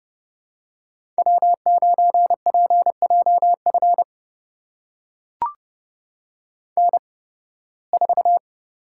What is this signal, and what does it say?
Morse code, heard as a keyed tone, about 32 wpm, spaced as for 20 wpm, 700 Hz. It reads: W9PJF N4
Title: USA Call Signs - 5 Characters - 8x Character Spacing - 30wpm